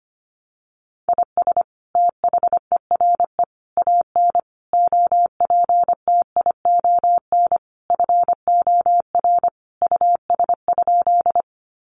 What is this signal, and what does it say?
Morse code, heard as a keyed tone, about 25 wpm, 700 Hz. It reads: IS THERE AN OPTION FOR US?
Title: Sentences from Top 1300 Words 25wpm